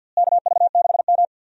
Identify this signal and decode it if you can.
Morse code, heard as a keyed tone, about 46 wpm, 700 Hz. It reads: KV6M
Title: USA Call Signs - 4 Characters - Repeated 2x - 50wpm